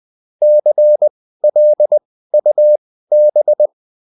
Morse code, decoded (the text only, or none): CLUB